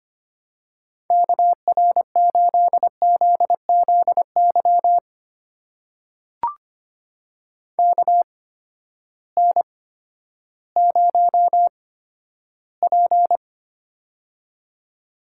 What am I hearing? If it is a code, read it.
Morse code, heard as a keyed tone, about 25 wpm, spaced as for 12 wpm, 700 Hz. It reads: KR8ZZY KN0P
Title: USA Call Signs - 8x Character Spacing - 25wpm